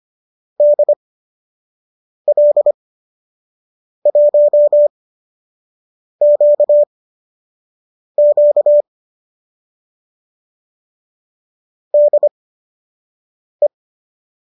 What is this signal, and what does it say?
Morse code, heard as a keyed tone, about 25 wpm, spaced as for 6 wpm, 590 Hz. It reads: DL1QQ DE
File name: qso012 06